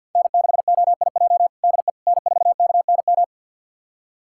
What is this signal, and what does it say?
Morse code, heard as a keyed tone, about 47 wpm, 700 Hz. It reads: N6OIJ DE N4KNM
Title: USA Callsign DE Callsign - 5 Characters - 50wpm